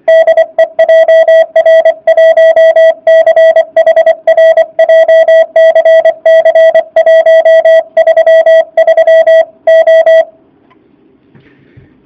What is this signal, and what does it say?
Morse code, heard as a keyed tone, about 24 wpm, 650 Hz. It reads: DE JR1CHR JCC 133O